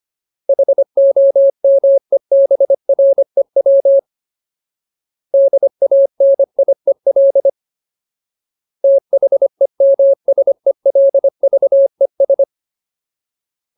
Morse code, heard as a keyed tone, about 25 wpm, 550 Hz. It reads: HOMEBREW DANIEL THEMSELVES
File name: Ultimate Word List / 25WPM Ultimate